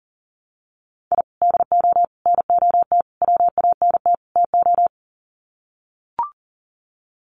Morse code, heard as a keyed tone, about 40 wpm, 700 Hz. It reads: I DO NOT WANT TO
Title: Sets of 5 Words 40wpm